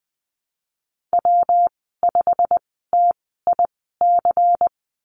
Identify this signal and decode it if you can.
Morse code, heard as a keyed tone, about 20 wpm, spaced as for 15 wpm, 700 Hz. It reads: W5TIC
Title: 128 Lesson 21g - Number 6 - Call Signs 20wpm